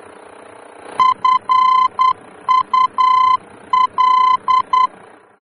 Morse code, heard as a keyed tone, about 10 wpm, 1020 Hz. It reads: FUL
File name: germany / mx / FUL-D-VOR